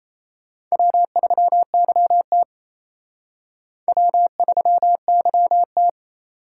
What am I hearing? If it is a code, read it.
Morse code, heard as a keyed tone, about 34 wpm, 700 Hz. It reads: W3YT W3YT
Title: Speed Racing - USA Call Signs 28wpm